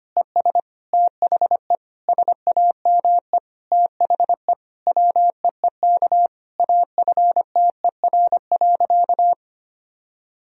Morse code, heard as a keyed tone, about 25 wpm, 700 Hz. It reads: ES THE SAME THE WEEK AFTER.
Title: Sentences from Top 500 Words and Abbreviations 25wpm